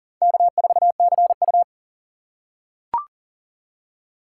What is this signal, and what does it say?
Morse code, heard as a keyed tone, about 39 wpm, 690 Hz. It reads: K4CU
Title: USA Call Signs - 4x Character Spacing - 40wpm